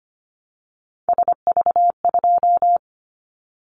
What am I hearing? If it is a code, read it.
Morse code, heard as a keyed tone, about 25 wpm, 700 Hz. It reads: SV2
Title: Callsign Prefixes for Top 50 DXCC - 25wpm